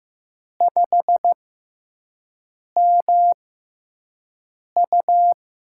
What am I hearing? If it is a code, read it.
Morse code, heard as a keyed tone, about 15 wpm, spaced as for 5 wpm, 700 Hz. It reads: 5MU
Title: USA Call Signs - 6x Character Spacing - 15wpm